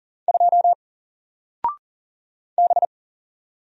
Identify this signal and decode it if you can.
Morse code, heard as a keyed tone, about 40 wpm, 700 Hz. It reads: 2 B
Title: Single Letter-Number 40wpm